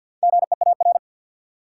Morse code, heard as a keyed tone, about 50 wpm, 700 Hz. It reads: GEAR
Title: Sentences from Top 1400 Words - New Word Drill Alt - 50wpm